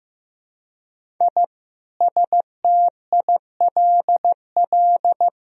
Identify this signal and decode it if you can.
Morse code, heard as a keyed tone, about 15 wpm, 700 Hz. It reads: I STILL
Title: Sentences from Top 1000 Words - Review Part 1 - 15wpm